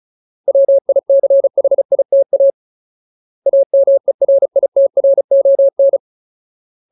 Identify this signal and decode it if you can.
Morse code, heard as a keyed tone, about 34 wpm, 550 Hz. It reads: WICHITA AMERITRON
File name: Ultimate Word List / 35WPM Ultimate